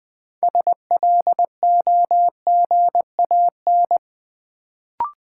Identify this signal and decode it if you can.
Morse code, heard as a keyed tone, about 20 wpm, 700 Hz. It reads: SLOGAN
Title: Build A Word - 2 Letters At A Time - 6-Letter Words - 20wpm